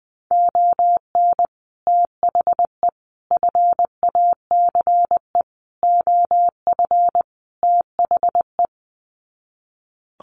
Morse code, heard as a keyed tone, about 20 wpm, 700 Hz. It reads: ON THE FACE OF THE
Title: Sets of 5 Words Encore2 20wpm